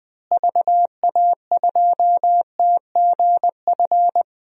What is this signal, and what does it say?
Morse code, heard as a keyed tone, about 20 wpm, 700 Hz. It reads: VA2TGF
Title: DXCC-100 Call Signs - 6 Characters - Repeated 3x - 20wpm